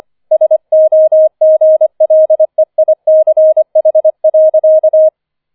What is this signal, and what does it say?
Morse code, heard as a keyed tone, about 25 wpm, 630 Hz. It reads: SOGLEICH.